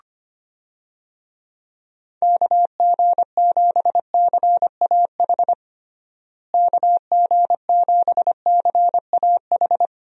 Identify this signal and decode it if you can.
Morse code, heard as a keyed tone, about 25 wpm, 700 Hz. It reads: KG7CAH KG7CAH
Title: USA Call Signs - 6 Characters - Repeated 2x - 25wpm